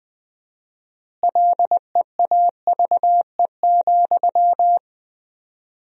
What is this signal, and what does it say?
Morse code, heard as a keyed tone, about 21 wpm, 700 Hz. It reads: LEAVE,
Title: 4x Word-Spacing - Sentences from Top 400 Words - 20wpm